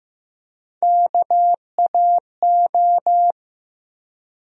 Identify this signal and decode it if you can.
Morse code, heard as a keyed tone, about 15 wpm, 700 Hz. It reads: KAO
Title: USA Call Sign Postfixes - 3 Letters 15wpm